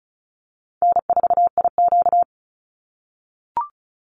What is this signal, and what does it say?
Morse code, heard as a keyed tone, about 34 wpm, 700 Hz. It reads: N4IQ
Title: Call Signs - 35wpm